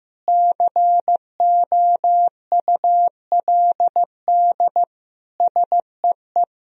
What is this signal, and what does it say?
Morse code, heard as a keyed tone, about 15 wpm, 700 Hz. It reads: COULD SEE
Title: Sentences from Top 1000 Words - Review Part 1 - 15wpm